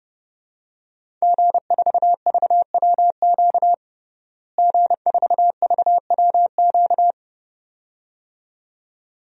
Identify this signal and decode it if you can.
Morse code, heard as a keyed tone, about 32 wpm, 700 Hz. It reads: G4VWQ G4VWQ
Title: DXCC-100 Call Signs - 5 Characters - Repeated 2x - 30wpm